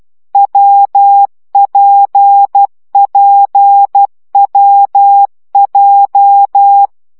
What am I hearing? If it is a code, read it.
Morse code, heard as a keyed tone, about 12 wpm, 800 Hz. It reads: WPPWJ
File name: CW gyakorlas w p j 10 baud